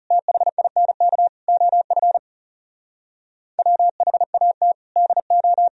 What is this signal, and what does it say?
Morse code, heard as a keyed tone, about 34 wpm, 700 Hz. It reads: THINK OF WHAT DO
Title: Speed-Racing - Sets of 5 Words 30wpm